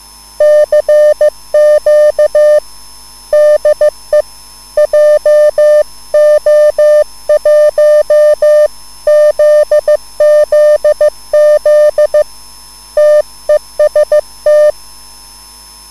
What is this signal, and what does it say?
Morse code, heard as a keyed tone, about 15 wpm, 600 Hz. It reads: CQ DE JO1ZZZ TEST
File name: QSL / CW lesson